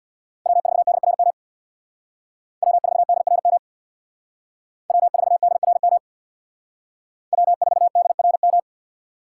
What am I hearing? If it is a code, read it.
Morse code, heard as a keyed tone, about 60 wpm, 700 Hz. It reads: W4DRM W4DRM W4DRM W4DRM
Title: Speed Racing - USA 1x3 Call Signs 50wpm